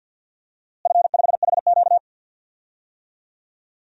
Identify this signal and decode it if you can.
Morse code, heard as a keyed tone, about 53 wpm, 700 Hz. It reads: F5HQ
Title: DXCC-100 Call Signs - 4 Characters - Repeated 2x - 50wpm